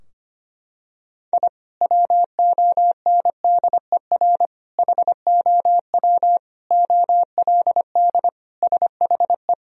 Morse code, heard as a keyed tone, about 25 wpm, 700 Hz. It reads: I WONDER HOW OLD SHE
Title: Sentences from Top 1000 Words - Review Part 3 - 25wpm